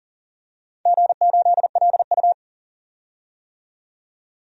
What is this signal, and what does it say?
Morse code, heard as a keyed tone, about 40 wpm, 700 Hz. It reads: G8LU
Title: DXCC-100 Call Signs - 4 Characters - 40wpm